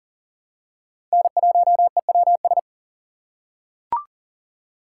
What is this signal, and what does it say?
Morse code, heard as a keyed tone, about 40 wpm, 700 Hz. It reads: N1EWS
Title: USA Call Signs - 5 Characters - 6x Character Spacing - 40wpm